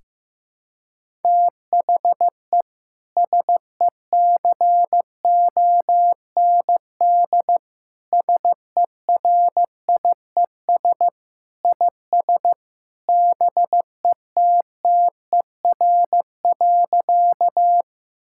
Morse code, heard as a keyed tone, about 16 wpm, 700 Hz. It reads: THE SECOND SERIES IS BETTER.
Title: Sentences from Top 1000 Words - Review Part 5 - 15wpm